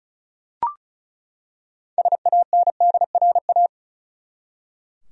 Morse code, heard as a keyed tone, about 36 wpm, 700 Hz. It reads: SANDRA
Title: USA Common Names - 35wpm